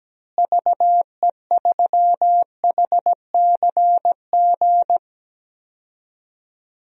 Morse code, heard as a keyed tone, about 17 wpm, 700 Hz. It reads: VE3HCG
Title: DXCC-100 Call Signs - 6 Characters - Repeated 3x - 17wpm